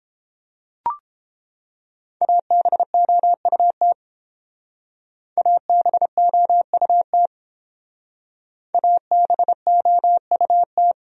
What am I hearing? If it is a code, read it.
Morse code, heard as a keyed tone, about 30 wpm, 700 Hz. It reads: ABOUT ABOUT ABOUT